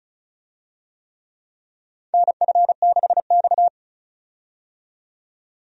Morse code, heard as a keyed tone, about 34 wpm, 700 Hz. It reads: NF6X
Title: USA Call Signs - 4 Characters - 35wpm